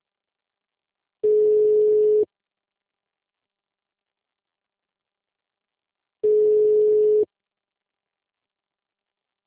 Ringback tone (Europe, 425 Hz).
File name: out- -2006- - - .2